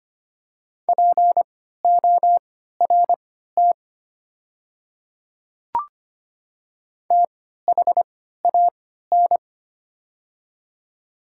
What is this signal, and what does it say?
Morse code, heard as a keyed tone, about 25 wpm, spaced as for 14 wpm, 700 Hz. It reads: PORT THAN
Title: 3x Character Spacing - Top 300 Words - 25wpm